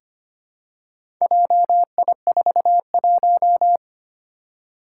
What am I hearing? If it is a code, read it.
Morse code, heard as a keyed tone, about 25 wpm, 700 Hz. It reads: JI41
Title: Four Challenging Letter-Number Combinations 25wpm